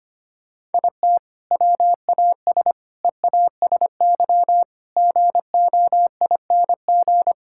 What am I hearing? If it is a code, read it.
Morse code, heard as a keyed tone, about 25 wpm, 700 Hz. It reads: IT WAS EASY GOING
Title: Sentences from Top 1000 Words - Review Part 5 - 25wpm